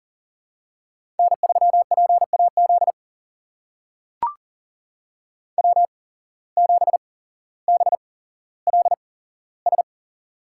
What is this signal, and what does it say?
Morse code, heard as a keyed tone, about 40 wpm, spaced as for 17 wpm, 700 Hz. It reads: N3PAZ W7BLS
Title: USA Call Signs - 5 Characters - 8x Character Spacing - 40wpm